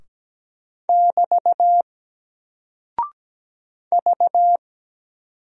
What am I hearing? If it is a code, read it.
Morse code, heard as a keyed tone, about 17 wpm, 700 Hz. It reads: <BT> V
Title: Single Character (Letter:Number:Punctuation:Prosign) 17wpm